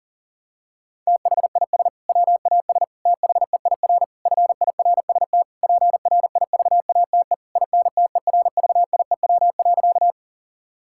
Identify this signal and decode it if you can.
Morse code, heard as a keyed tone, about 40 wpm, 700 Hz. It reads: THIS WAS THEIR FIRST PRIVATE INTERVIEW.
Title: Sentences from Top 1600 Words 40wpm